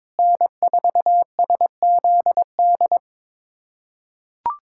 Morse code, heard as a keyed tone, about 22 wpm, 700 Hz. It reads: N4SZD
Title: USA Callsign DE Callsign - 5 Characters - 22wpm